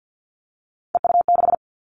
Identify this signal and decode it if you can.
Morse code, heard as a keyed tone, about 50 wpm, 700 Hz. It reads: EU6
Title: Callsign Prefixes for Top 100 DXCC - 50wpm